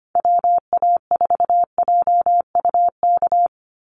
Morse code, heard as a keyed tone, about 25 wpm, 700 Hz. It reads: WA4JUK
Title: Call Signs Repeated 3x - 25wpm